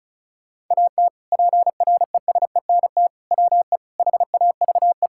Morse code, heard as a keyed tone, about 36 wpm, 700 Hz. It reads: AT PRESENT WE HAVE
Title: Sentences from Top 1000 Words - Review Part 5 - 35wpm